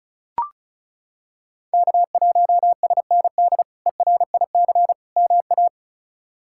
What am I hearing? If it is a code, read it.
Morse code, heard as a keyed tone, about 36 wpm, 700 Hz. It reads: K1SND ERIC MA
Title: K1USN SST QSO Practice - 35wpm